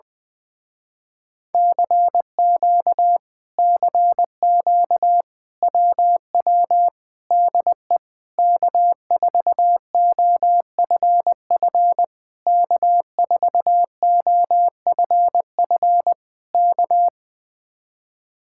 Morse code, heard as a keyed tone, about 20 wpm, 700 Hz. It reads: CQ CQ WW DE K4OFF K4OFF K
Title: Speed-Racing Common QSO Phrases - 20wpm